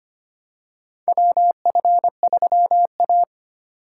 Morse code, heard as a keyed tone, about 25 wpm, 700 Hz. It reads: WF3A
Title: USA Call Signs - 4x Character Spacing - 25wpm